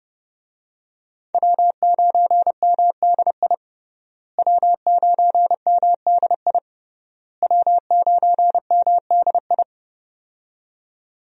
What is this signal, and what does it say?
Morse code, heard as a keyed tone, about 30 wpm, 700 Hz. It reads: W9MDI W9MDI W9MDI
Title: USA Call Signs - 5 Characters - Repeated 3x - 30wpm